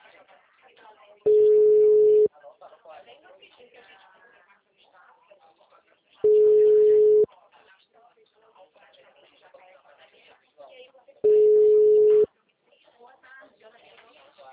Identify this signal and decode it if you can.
Ringback tone (Europe, 425 Hz), with speech.